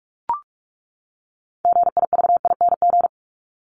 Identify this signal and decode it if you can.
Morse code, heard as a keyed tone, about 44 wpm, 700 Hz. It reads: GIVING
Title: ING Suffix - Top 100 Words with Base Words 45wpm